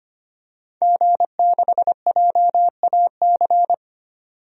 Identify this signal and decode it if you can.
Morse code, heard as a keyed tone, about 25 wpm, 700 Hz. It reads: G6JAC
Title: DXCC-100 Call Signs - 5 Characters - 25wpm